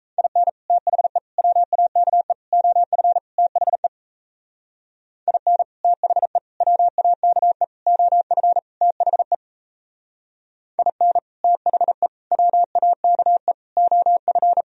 Morse code, heard as a keyed tone, about 38 wpm, 700 Hz. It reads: IN THE WAKE OF THE IN THE WAKE OF THE IN THE WAKE OF